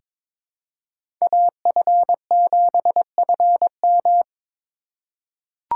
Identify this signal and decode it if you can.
Morse code, heard as a keyed tone, about 22 wpm, 700 Hz. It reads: AF7FM
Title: USA Call Signs - 5 Characters - 4x Character Spacing - 22wpm